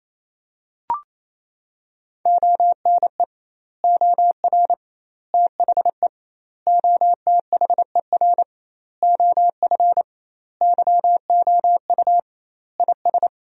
Morse code, heard as a keyed tone, about 28 wpm, 700 Hz. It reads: ONE OR THE OTHER OF YOU IS
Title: 2x Word-Spacing - Sentences from Top 400 Words - 28wpm